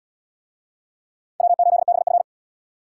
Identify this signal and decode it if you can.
Morse code, heard as a keyed tone, about 72 wpm, 700 Hz. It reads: W8BF